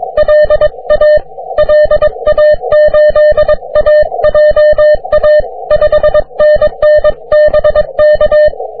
Morse code, heard as a keyed tone, about 24 wpm, 600 Hz. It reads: LA LA8AJA 5NN BK
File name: wp / tx5spa 40m cw